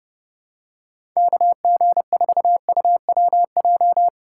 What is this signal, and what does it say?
Morse code, heard as a keyed tone, about 30 wpm, 700 Hz. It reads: KG4UWJ
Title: USA Call Signs - 6 Characters - 3x Character Spacing - 30wpm